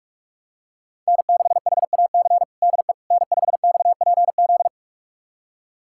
Morse code, heard as a keyed tone, about 46 wpm, 700 Hz. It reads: N6HAC DE N5XPZ
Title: USA Callsign DE Callsign - 5 Characters - 45wpm